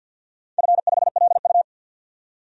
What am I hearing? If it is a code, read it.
Morse code, heard as a keyed tone, about 46 wpm, 700 Hz. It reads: F5LU
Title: DXCC-100 Call Signs - 4 Characters - 50wpm